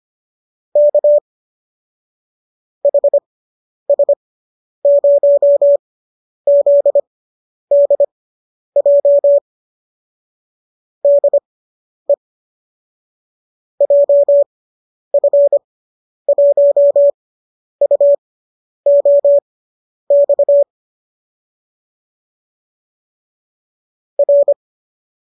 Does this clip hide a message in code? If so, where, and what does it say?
Yes: Morse code from 0.8 to 24.5 s, heard as a keyed tone, about 25 wpm, spaced as for 10 wpm, 570 Hz. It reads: K HS0ZDJ DE JF1UOX R